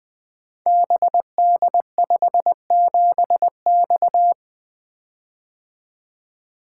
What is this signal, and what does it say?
Morse code, heard as a keyed tone, about 21 wpm, 700 Hz. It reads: BD57X
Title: Code Groups - 5 Letter:Numbers - 20wpm